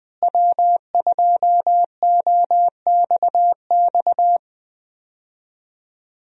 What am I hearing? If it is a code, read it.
Morse code, heard as a keyed tone, about 20 wpm, 700 Hz. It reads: W2OXX
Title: USA Call Signs - 5 Characters - Repeated 2x - 20wpm